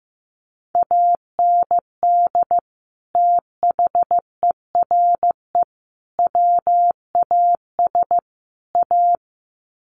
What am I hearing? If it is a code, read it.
Morse code, heard as a keyed tone, about 15 wpm, 700 Hz. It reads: AND THERE WAS A
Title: Sets of 4 Words 15wpm